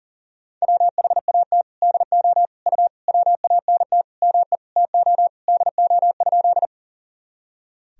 Morse code, heard as a keyed tone, about 38 wpm, 700 Hz. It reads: WHAT DO U WANT ME TO DO?
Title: Sentences from Top 200 Words and Abbreviations 40wpm